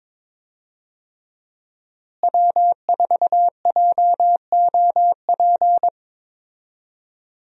Morse code, heard as a keyed tone, about 22 wpm, 700 Hz. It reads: W4JOP
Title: USA Call Signs - 5 Characters - Repeated 3x - 22wpm